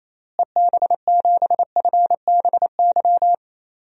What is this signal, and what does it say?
Morse code, heard as a keyed tone, about 28 wpm, 700 Hz. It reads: EB7FBY
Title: DXCC-100 Call Signs - 6 Characters - Repeated 3x - 28wpm